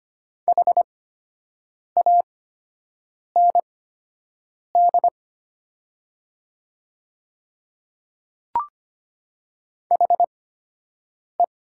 Morse code, heard as a keyed tone, about 25 wpm, spaced as for 7 wpm, 700 Hz. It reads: HAND HE
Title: 8x Character Spacing - Top 200 Words - 25wpm